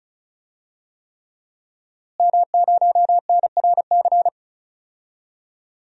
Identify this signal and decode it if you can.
Morse code, heard as a keyed tone, about 36 wpm, 700 Hz. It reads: M0NRC
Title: DXCC-100 Call Signs - 5 Characters - Repeated 3x - 35wpm